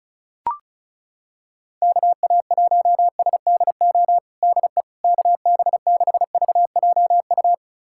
Morse code, heard as a keyed tone, about 36 wpm, 700 Hz. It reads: KA1SDO DE KB6VJU
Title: USA Callsign DE Callsign - 6 Characters - 35wpm